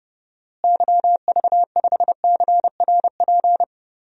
Morse code, heard as a keyed tone, about 30 wpm, 700 Hz. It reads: YV5CRP
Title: DXCC-100 Call Signs - 6 Characters - Repeated 3x - 30wpm